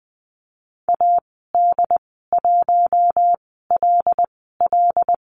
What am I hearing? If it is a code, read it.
Morse code, heard as a keyed tone, about 20 wpm, spaced as for 14 wpm, 700 Hz. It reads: AD1LL
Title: 076 Lesson 13g - Letter L - Call Signs 20wpm